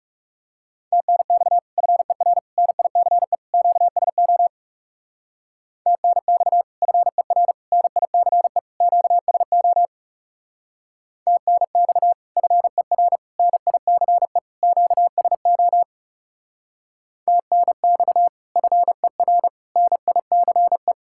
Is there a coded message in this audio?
Yes: Morse code from 0.9 to 20.9 s, heard as a keyed tone, about 40 wpm, 700 Hz. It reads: TNX FER NICE QSO TNX FER NICE QSO TNX FER NICE QSO TNX FER NICE